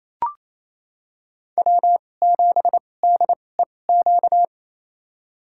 Morse code, heard as a keyed tone, about 28 wpm, spaced as for 20 wpm, 700 Hz. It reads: W7DEQ
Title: USA Call Signs - 5 Characters - 2x Character Spacing - 28wpm